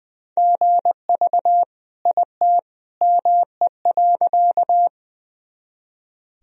Morse code, heard as a keyed tone, about 20 wpm, 700 Hz. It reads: GV IT ME.
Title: Sentences from Top 200 Words and Abbreviations 20wpm